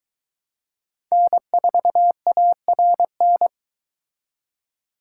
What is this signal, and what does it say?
Morse code, heard as a keyed tone, about 23 wpm, 700 Hz. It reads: N4ARN